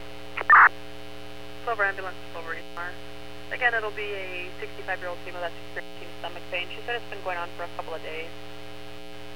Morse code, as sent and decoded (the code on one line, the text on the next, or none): none